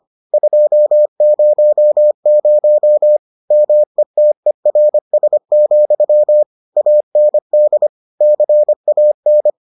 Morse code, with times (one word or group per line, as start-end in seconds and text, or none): 0.3-3.2 200
3.5-6.4 METERS,
6.8-7.9 AND
8.2-9.5 CAN